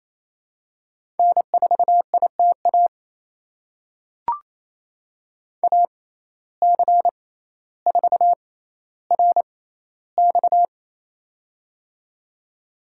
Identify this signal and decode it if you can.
Morse code, heard as a keyed tone, about 28 wpm, spaced as for 14 wpm, 700 Hz. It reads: N4ITA AC4RX
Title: USA Call Signs - 5 Characters - 6x Character Spacing - 28wpm